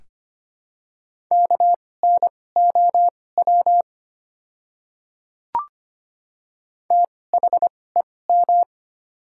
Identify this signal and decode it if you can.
Morse code, heard as a keyed tone, about 25 wpm, spaced as for 18 wpm, 700 Hz. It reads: KNOW THEM
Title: 2x Character Spacing - Top 100 Words - 25wpm